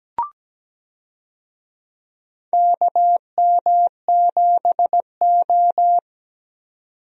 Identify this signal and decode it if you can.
Morse code, heard as a keyed tone, about 17 wpm, 700 Hz. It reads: KM7O
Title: USA Call Signs - 4 Characters - 17wpm